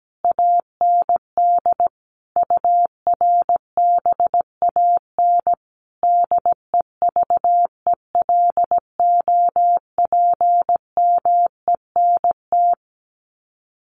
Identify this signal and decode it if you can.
Morse code, heard as a keyed tone, about 17 wpm, 700 Hz. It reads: AND URBAN DEVELOPMENT
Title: Sets of 5 Words Encore2 17wpm